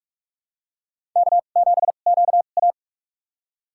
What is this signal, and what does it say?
Morse code, heard as a keyed tone, about 44 wpm, spaced as for 32 wpm, 700 Hz. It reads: K7QA
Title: USA Call Signs - 4 Characters - 2x Character Spacing - 45wpm